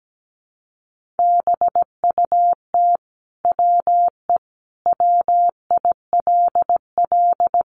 Morse code, heard as a keyed tone, about 17 wpm, 700 Hz. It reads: BUT WE WILL
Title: Sentences from Top 1500 Words 17wpm